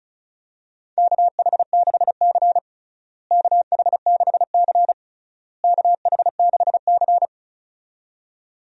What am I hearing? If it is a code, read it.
Morse code, heard as a keyed tone, about 34 wpm, 700 Hz. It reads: KH6C KH6C KH6C
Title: DXCC-100 Call Signs - 4 Characters - Repeated 3x - 35wpm